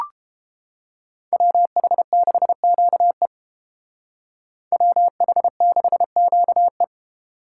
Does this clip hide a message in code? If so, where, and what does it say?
Yes: Morse code from 1.3 to 6.8 s, heard as a keyed tone, about 31 wpm, 700 Hz. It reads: WH6QE WH6QE